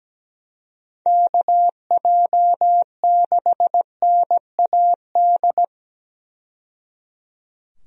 Morse code, heard as a keyed tone, about 17 wpm, 700 Hz. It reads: KJ6NAD
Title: USA Call Signs - 6 Characters - 17wpm